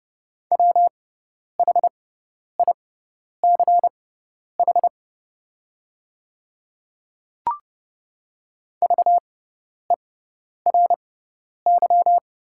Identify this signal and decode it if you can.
Morse code, heard as a keyed tone, about 30 wpm, spaced as for 10 wpm, 700 Hz. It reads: WHICH VERY
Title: 6x Character Spacing - Top 200 Words - 30wpm